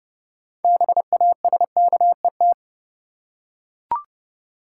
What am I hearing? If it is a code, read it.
Morse code, heard as a keyed tone, about 30 wpm, 700 Hz. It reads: BASKET
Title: Six Letter Words - 30wpm